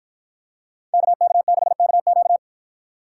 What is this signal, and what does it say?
Morse code, heard as a keyed tone, about 53 wpm, 700 Hz. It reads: KK6KQ